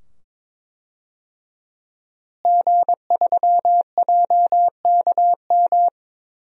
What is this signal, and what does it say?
Morse code, heard as a keyed tone, about 21 wpm, 700 Hz. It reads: G3JKM
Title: DXCC-100 Call Signs - 5 Characters - Repeated 2x - 22wpm